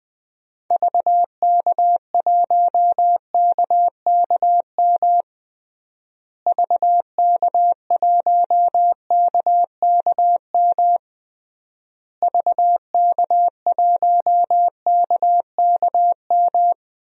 Morse code, heard as a keyed tone, about 20 wpm, 700 Hz. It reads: VK1KKM VK1KKM VK1KKM
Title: DXCC-100 Call Signs - 6 Characters - Repeated 3x - 20wpm